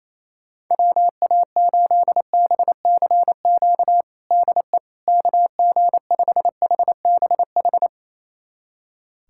Morse code, heard as a keyed tone, about 28 wpm, 700 Hz. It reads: WA8BCQ DE KG5HBH
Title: USA Callsign DE Callsign - 6 Characters - 28wpm